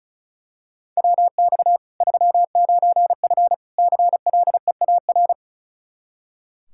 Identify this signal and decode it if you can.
Morse code, heard as a keyed tone, about 36 wpm, 700 Hz. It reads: WX 39F CLEAR
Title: Practice QSO - Exchange 2 - Info - 35wpm